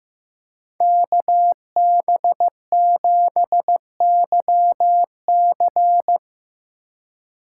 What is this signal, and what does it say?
Morse code, heard as a keyed tone, about 18 wpm, 700 Hz. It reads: KB7YC
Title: Speed Racing - USA 2x2 Call Signs 15wpm